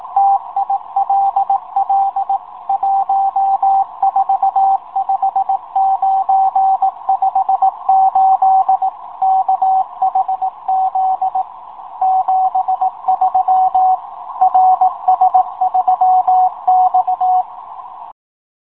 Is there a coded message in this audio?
Yes: Morse code from 0.2 to 17.4 s, heard as a keyed tone, about 18 wpm, 780 Hz. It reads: TILL 145958 KHZ 73 RS3X